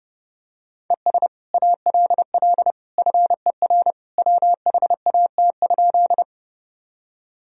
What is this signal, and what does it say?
Morse code, heard as a keyed tone, about 30 wpm, 700 Hz. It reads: ES ALL FER WHAT?
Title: Sentences from Top 100 Words and Abbreviations 30wpm